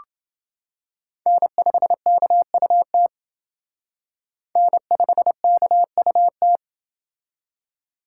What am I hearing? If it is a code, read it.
Morse code, heard as a keyed tone, about 28 wpm, 700 Hz. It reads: N5KUT N5KUT